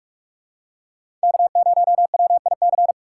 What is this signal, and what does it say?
Morse code, heard as a keyed tone, about 44 wpm, 700 Hz. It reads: K0WIC
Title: USA Call Signs - 5 Characters - 4x Character Spacing - 45wpm